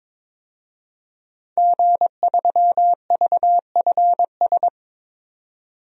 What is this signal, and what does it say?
Morse code, heard as a keyed tone, about 22 wpm, 700 Hz. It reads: G3VFS
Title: DXCC-100 Call Signs - 5 Characters - 22wpm